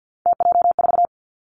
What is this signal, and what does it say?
Morse code, heard as a keyed tone, about 50 wpm, 700 Hz. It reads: TJ4
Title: Three Challenging Letter-Number Combinations 50wpm